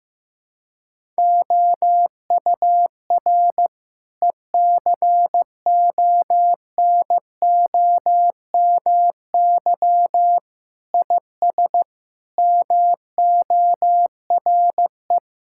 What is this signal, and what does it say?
Morse code, heard as a keyed tone, about 15 wpm, 700 Hz. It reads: OUR ECONOMY IS MORE
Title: Sentences from Top 1000 Words - Review Part 6 - 15wpm